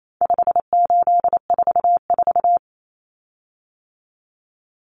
Morse code, heard as a keyed tone, about 28 wpm, 700 Hz. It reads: 5844
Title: Common Cut Numbers - 4-Digit 28wpm